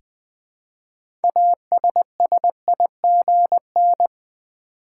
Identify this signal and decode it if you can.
Morse code, heard as a keyed tone, about 20 wpm, 700 Hz. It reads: ASSIGN
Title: 6 Letter Words - First 3 Letters Given - 20wpm